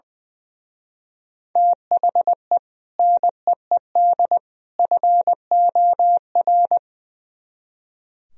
Morse code, heard as a keyed tone, about 22 wpm, 700 Hz. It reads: THE NEED FOR
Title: Speed-Racing - Sets of 3 Words 20wpm